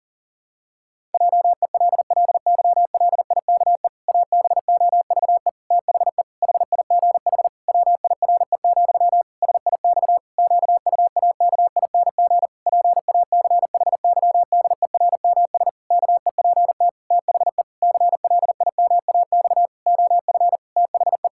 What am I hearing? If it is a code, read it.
Morse code, heard as a keyed tone, about 37 wpm, 700 Hz. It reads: JELLYLIKE ABOVE THE HIGH WIRE, SIX QUAKING PACHYDERMS KEPT THE CLIMAX OF THE
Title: Pangram Challenge v2 - 40wpm